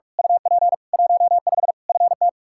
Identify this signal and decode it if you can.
Morse code, heard as a keyed tone, about 46 wpm, 700 Hz. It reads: UP 15 FT
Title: Practice QSO - Exchange 2 - Info - 45wpm